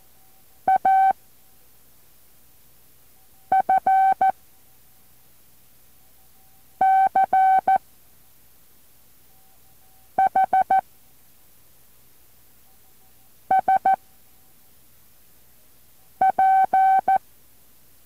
Morse code, heard as a keyed tone, about 14 wpm, 770 Hz. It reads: A F C H S P
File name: intl morse code2